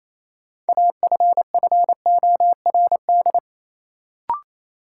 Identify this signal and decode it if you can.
Morse code, heard as a keyed tone, about 27 wpm, 700 Hz. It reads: AFFORD
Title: Sentences from Top 1100 Words - New Word Drill - 28wpm